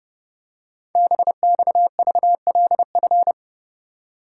Morse code, heard as a keyed tone, about 30 wpm, 700 Hz. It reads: BXVLF
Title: Code Groups - 5 Letters - 30wpm